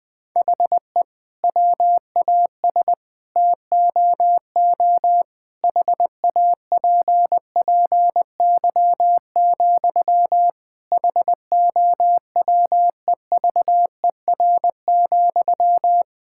Morse code, heard as a keyed tone, about 20 wpm, 700 Hz. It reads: HE WAS TOO HAPPY, HOWEVER,
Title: Sentences from Top 1000 Words - Review Part 6 - 20wpm